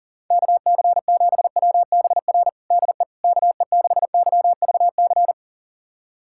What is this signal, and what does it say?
Morse code, heard as a keyed tone, about 40 wpm, 700 Hz. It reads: KC7WBR DE KE6YVC
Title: USA Callsign DE Callsign - 6 Characters - 40wpm